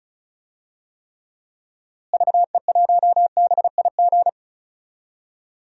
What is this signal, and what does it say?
Morse code, heard as a keyed tone, about 34 wpm, 700 Hz. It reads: VE1BIG
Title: DXCC-100 Call Signs - 6 Characters - Repeated 3x - 35wpm